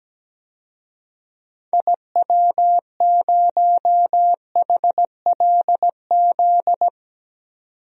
Morse code, heard as a keyed tone, about 17 wpm, 700 Hz. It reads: IW0HLZ
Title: DXCC-100 Call Signs - 6 Characters - 17wpm